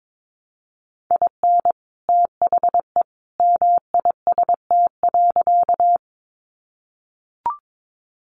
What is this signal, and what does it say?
Morse code, heard as a keyed tone, about 22 wpm, 700 Hz. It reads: IN THE MIST.
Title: A Sound Of Thunder 22wpm